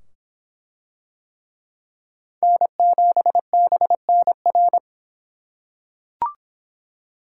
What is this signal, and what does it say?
Morse code, heard as a keyed tone, about 24 wpm, 700 Hz. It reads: N7BNR